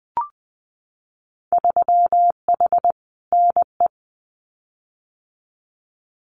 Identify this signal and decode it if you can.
Morse code, heard as a keyed tone, about 20 wpm, 700 Hz. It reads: 3H NE
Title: Winter Field Day Exchanges Encore 20wpm